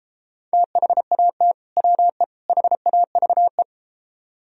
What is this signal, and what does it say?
Morse code, heard as a keyed tone, about 33 wpm, 700 Hz. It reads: THAT WE HAVE